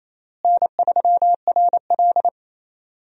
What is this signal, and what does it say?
Morse code, heard as a keyed tone, about 28 wpm, 700 Hz. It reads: N3RL
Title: USA Call Signs - 4 Characters - Repeated 3x - 28wpm